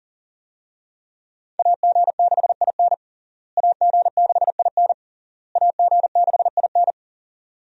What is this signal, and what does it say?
Morse code, heard as a keyed tone, about 40 wpm, 700 Hz. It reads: AG6IN AG6IN AG6IN
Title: USA Call Signs - 5 Characters - Repeated 3x - 40wpm